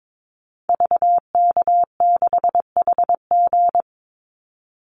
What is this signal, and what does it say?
Morse code, heard as a keyed tone, about 22 wpm, 700 Hz. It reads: VK6HG
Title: Call Signs Repeated 2x - 22wpm